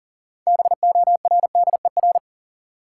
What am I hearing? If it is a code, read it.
Morse code, heard as a keyed tone, about 40 wpm, 700 Hz. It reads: BORDER
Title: Six Letter Words - 40wpm